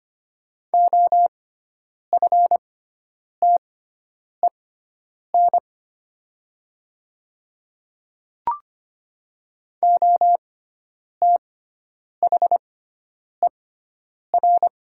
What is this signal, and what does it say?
Morse code, heard as a keyed tone, about 25 wpm, spaced as for 9 wpm, 700 Hz. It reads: OFTEN OTHER
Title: 6x Character Spacing - Top 300 Words - 25wpm